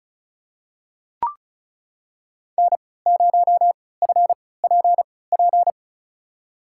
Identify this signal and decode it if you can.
Morse code, heard as a keyed tone, about 34 wpm, spaced as for 20 wpm, 700 Hz. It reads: N0FPP
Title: USA Call Signs - 3x Character Spacing - 35wpm